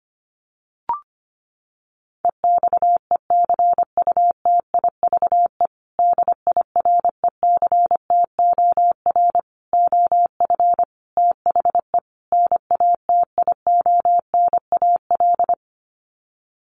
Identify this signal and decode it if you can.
Morse code, heard as a keyed tone, about 25 wpm, 700 Hz. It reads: EXECUTIVE DIRECTOR OF THE NATIONAL
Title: Sets of 5 Words Encore2 25wpm